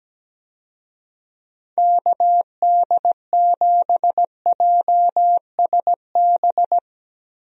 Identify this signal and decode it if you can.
Morse code, heard as a keyed tone, about 17 wpm, 700 Hz. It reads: KD7JSB
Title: USA Call Signs - 6 Characters - Repeated 3x - 17wpm